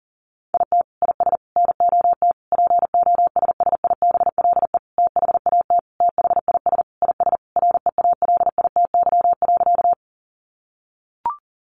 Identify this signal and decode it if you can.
Morse code, heard as a keyed tone, about 40 wpm, 700 Hz. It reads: IT IS NOT POSSIBLE THAT THIS IS REALITY.
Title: Sentences from Top 1200 Words 40wpm